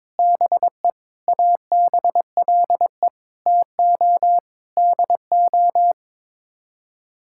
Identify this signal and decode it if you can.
Morse code, heard as a keyed tone, about 22 wpm, 700 Hz. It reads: BE ABLE TO DO
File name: practice / 22wpm / Build A Phrase - 5-Word Phrases - 22wpm